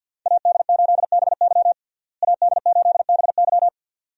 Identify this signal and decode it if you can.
Morse code, heard as a keyed tone, about 50 wpm, 700 Hz. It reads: AD8BY AD8BY
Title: USA Call Signs - 5 Characters - Repeated 2x - 50wpm